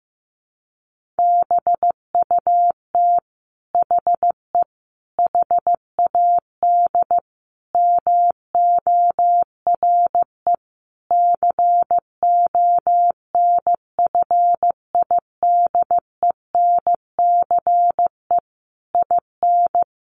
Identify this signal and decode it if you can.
Morse code, heard as a keyed tone, about 15 wpm, 700 Hz. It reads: BUT HE HAD MORE CONFIDENCE IN
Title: Sentences from Top 1500 Words 15wpm